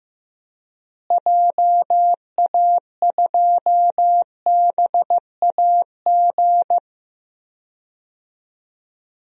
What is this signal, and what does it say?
Morse code, heard as a keyed tone, about 15 wpm, 700 Hz. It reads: JA2BAG
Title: DXCC-100 Call Signs - 6 Characters - Repeated 3x - 15wpm